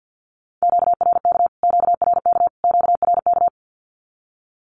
Morse code, heard as a keyed tone, about 48 wpm, 700 Hz. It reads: QRK QRK QRK
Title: Q-Codes Repeated 3x 50wpm